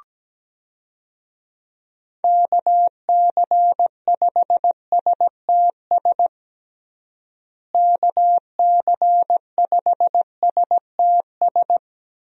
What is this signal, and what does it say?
Morse code, heard as a keyed tone, about 17 wpm, 700 Hz. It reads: KC5STS KC5STS
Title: USA Call Signs - 6 Characters - Repeated 3x - 17wpm